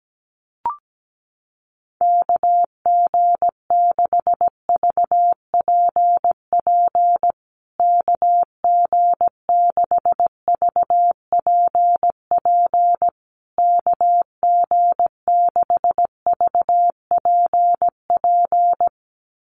Morse code, heard as a keyed tone, about 17 wpm, 700 Hz. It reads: KG6VPP KG6VPP KG6VPP
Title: USA Call Signs Repeated 3x 17wpm